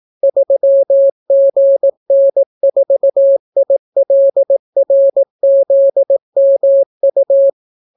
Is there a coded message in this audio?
Yes: Morse code from 0.2 to 7.5 s, heard as a keyed tone, about 18 wpm, 550 Hz. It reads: 3GN4ILRZMU